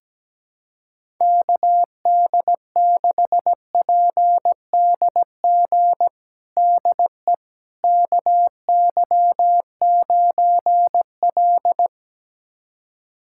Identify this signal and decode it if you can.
Morse code, heard as a keyed tone, about 17 wpm, 700 Hz. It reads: KD6PDG DE KY9L
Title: USA Callsign DE Callsign 17wpm